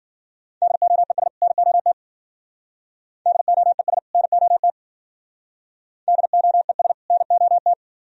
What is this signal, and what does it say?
Morse code, heard as a keyed tone, about 54 wpm, 700 Hz. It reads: DOES NOT DOES NOT DOES NOT